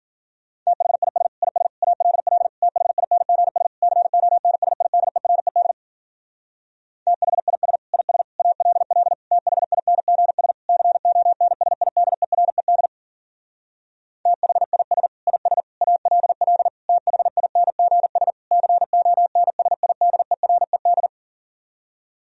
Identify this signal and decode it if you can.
Morse code, heard as a keyed tone, about 53 wpm, 700 Hz. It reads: THIS IS ALL THINGS CONSIDERED THIS IS ALL THINGS CONSIDERED THIS IS ALL THINGS CONSIDERED